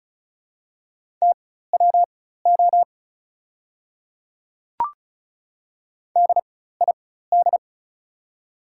Morse code, heard as a keyed tone, about 36 wpm, spaced as for 16 wpm, 690 Hz. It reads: TWO DID
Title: 4x Character Spacing - Top 300 Words - 35wpm